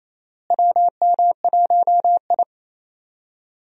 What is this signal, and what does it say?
Morse code, heard as a keyed tone, about 28 wpm, 700 Hz. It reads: WM1I
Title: USA Call Signs - 4 Characters - Repeated 3x - 28wpm